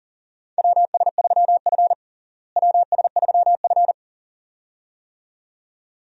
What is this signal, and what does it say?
Morse code, heard as a keyed tone, about 40 wpm, 700 Hz. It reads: WS3F WS3F
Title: USA Call Signs - 4 Characters - Repeated 2x - 40wpm